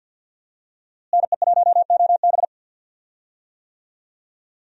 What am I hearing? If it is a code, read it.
Morse code, heard as a keyed tone, about 52 wpm, 700 Hz. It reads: NE1OB
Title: USA Call Signs - 5 Characters - Repeated 2x - 50wpm